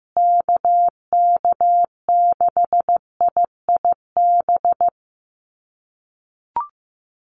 Morse code, heard as a keyed tone, about 15 wpm, 700 Hz. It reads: KK6IIB
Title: USA Call Signs Repeated 2x 15wpm